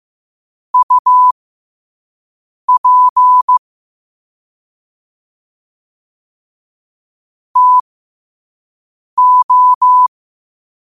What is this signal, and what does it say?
Morse code, heard as a keyed tone, about 15 wpm, spaced as for 5 wpm, 1000 Hz. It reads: UP TO
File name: qotdfiles / 5wpm